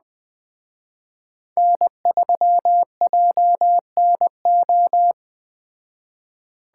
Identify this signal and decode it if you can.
Morse code, heard as a keyed tone, about 20 wpm, 700 Hz. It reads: N3JNO
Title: Speed Racing - USA Call Signs 17wpm